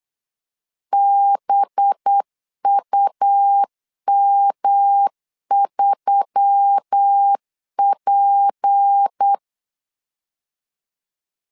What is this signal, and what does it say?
Morse code, heard as a keyed tone, about 8 wpm, 800 Hz. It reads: BUM3P
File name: LEZIONI / 8 YJ45 / 8 Y 8wpm 3245